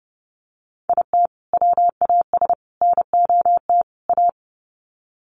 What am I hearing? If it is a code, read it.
Morse code, heard as a keyed tone, about 30 wpm, 700 Hz. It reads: IT WAS NOT A
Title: Sets of 4 Words 30wpm